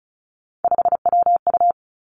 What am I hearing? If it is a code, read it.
Morse code, heard as a keyed tone, about 36 wpm, 700 Hz. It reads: 5WU
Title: Three Challenging Letter-Number Combinations 35wpm